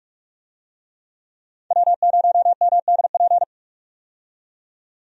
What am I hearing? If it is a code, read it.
Morse code, heard as a keyed tone, about 43 wpm, 700 Hz. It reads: W0MDP
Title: USA Call Signs - 5 Characters - 45wpm